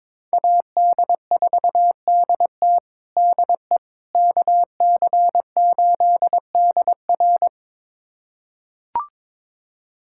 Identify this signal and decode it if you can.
Morse code, heard as a keyed tone, about 22 wpm, 700 Hz. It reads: AD4DT DE KC8DR
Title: USA Callsign DE Callsign - 5 Characters - 22wpm